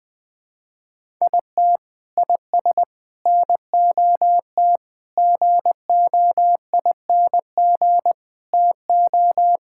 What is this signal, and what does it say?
Morse code, heard as a keyed tone, about 20 wpm, 700 Hz. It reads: IT IS NOT GOING TO
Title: Sentences with 4-Word Phrases - 20wpm